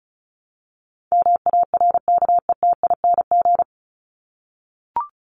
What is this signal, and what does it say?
Morse code, heard as a keyed tone, about 34 wpm, 700 Hz. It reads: MARKETING
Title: Top 751-1000 Words 35wpm